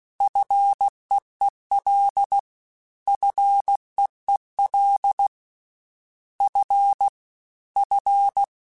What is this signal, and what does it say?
Morse code, heard as a keyed tone, about 16 wpm, 800 Hz. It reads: FEEL FEEL F F
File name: F and L / F and L mp3 / F and L Prefixes 3x2 Twice @15